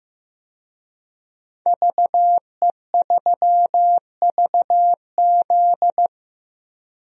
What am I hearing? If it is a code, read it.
Morse code, heard as a keyed tone, about 15 wpm, 700 Hz. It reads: VE3VZ
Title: DXCC-100 Call Signs - 5 Characters - Repeated 2x - 15wpm